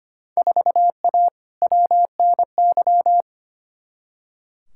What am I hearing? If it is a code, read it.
Morse code, heard as a keyed tone, about 25 wpm, 700 Hz. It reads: 4A WNY
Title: Field Day Exchanges Encore - 25wpm